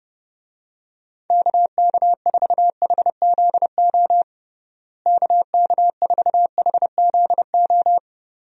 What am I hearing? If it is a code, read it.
Morse code, heard as a keyed tone, about 30 wpm, 700 Hz. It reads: KK4HZO KK4HZO
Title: USA Call Signs - 6 Characters - Repeated 2x - 30wpm